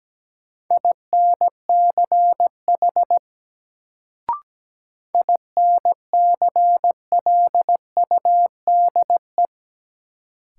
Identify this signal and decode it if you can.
Morse code, heard as a keyed tone, about 17 wpm, 700 Hz. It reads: INCH INCLUDE
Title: Sentences from Top 1000 Words - Review Part 3 - New Word Drill - 17wpm